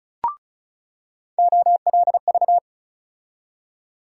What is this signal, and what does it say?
Morse code, heard as a keyed tone, about 34 wpm, 700 Hz. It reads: OLV
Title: USA Call Sign Postfixes - 3 Letters 35wpm